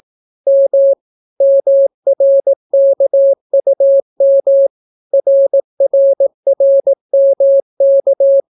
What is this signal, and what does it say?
Morse code, heard as a keyed tone, about 18 wpm, 550 Hz. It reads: M MRKUM RRRMK